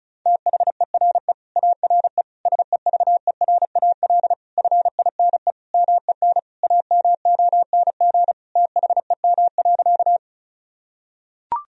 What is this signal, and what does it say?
Morse code, heard as a keyed tone, about 36 wpm, 700 Hz. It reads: THERE ARE SEVERAL FINE MEN AMONG THEM.
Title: Sentences from Top 1000 Words - Review Part 2 - 35wpm